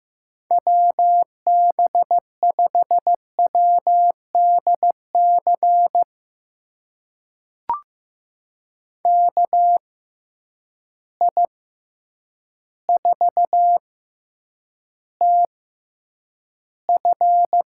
Morse code, heard as a keyed tone, about 15 wpm, spaced as for 8 wpm, 700 Hz. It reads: WB5WDC KI4TF
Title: USA Call Signs - 6 Characters - 6x Character Spacing - 15wpm